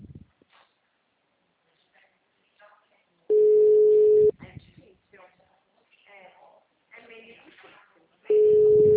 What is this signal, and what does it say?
Ringback tone (Europe, 425 Hz), with speech.